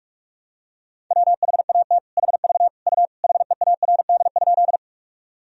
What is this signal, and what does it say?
Morse code, heard as a keyed tone, about 46 wpm, 700 Hz. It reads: WHAT HV U HEARD?
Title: Sentences from Top 400 Words and Abbreviations 45wpm